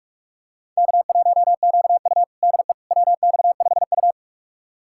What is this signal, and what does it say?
Morse code, heard as a keyed tone, about 42 wpm, 700 Hz. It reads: K1QU DE WX5U
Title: USA Callsign DE Callsign - 4 Characters - 45wpm